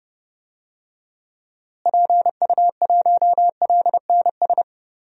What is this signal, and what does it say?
Morse code, heard as a keyed tone, about 32 wpm, 700 Hz. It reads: PU1LNS
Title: DXCC-100 Call Signs - 6 Characters - Repeated 3x - 30wpm